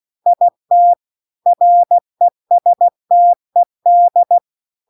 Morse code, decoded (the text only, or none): IT RESTED